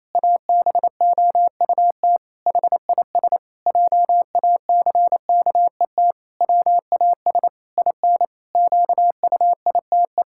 Morse code, heard as a keyed tone, about 27 wpm, 700 Hz. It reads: ABOUT HIS JACKET WAS IN QUITE
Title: Pangram Challenge v2 - 28wpm